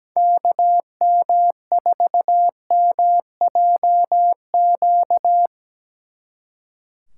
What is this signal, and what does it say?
Morse code, heard as a keyed tone, about 20 wpm, 700 Hz. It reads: KM4MJQ
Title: Speed Racing - USA 2x3 Call Signs 17wpm